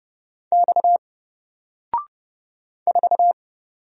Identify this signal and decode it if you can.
Morse code, heard as a keyed tone, about 29 wpm, 700 Hz. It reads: X 4
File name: Single-Letter-Number-30wpm